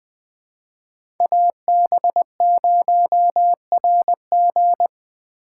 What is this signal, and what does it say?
Morse code, heard as a keyed tone, about 20 wpm, 700 Hz. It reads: AB0RG
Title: USA Call Signs - 3x Character Spacing - 20wpm